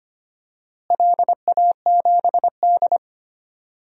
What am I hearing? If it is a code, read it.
Morse code, heard as a keyed tone, about 25 wpm, 700 Hz. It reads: LA7D
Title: DXCC-100 Call Signs - 4 Characters - Repeated 2x - 25wpm